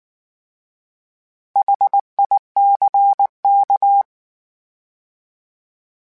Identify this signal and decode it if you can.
Morse code, heard as a keyed tone, about 19 wpm, 800 Hz. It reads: HICK
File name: H Prefixes / H mp3s